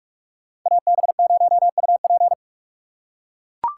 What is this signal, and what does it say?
Morse code, heard as a keyed tone, about 44 wpm, 700 Hz. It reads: AB0UP